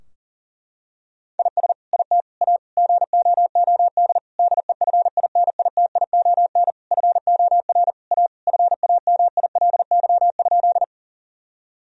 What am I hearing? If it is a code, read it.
Morse code, heard as a keyed tone, about 40 wpm, 700 Hz. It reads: IS IT A GOOD DEFINITION FOR A FAMILY?
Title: Sentences from Top 1000 Words - Review Part 4 - 40wpm